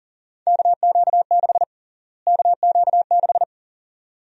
Morse code, heard as a keyed tone, about 40 wpm, 700 Hz. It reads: KQ6 KQ6
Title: USA Call Sign Prefixes - Repeated 2x - 40wpm